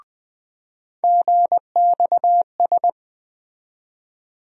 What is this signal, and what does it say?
Morse code, heard as a keyed tone, about 21 wpm, 700 Hz. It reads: GXS
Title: USA Call Sign Postfixes - 3 Letters 20wpm